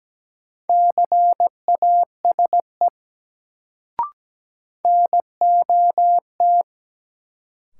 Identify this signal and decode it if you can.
Morse code, heard as a keyed tone, about 17 wpm, 700 Hz. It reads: CASE NOT
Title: Sets of 2 Words - New Word Drill - 17wpm